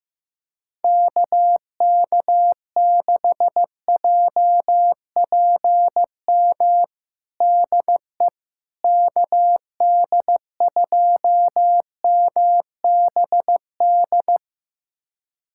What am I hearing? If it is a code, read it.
Morse code, heard as a keyed tone, about 15 wpm, 700 Hz. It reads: KK6JPM DE KD2MBD
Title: USA Callsign DE Callsign - 6 Characters - 15wpm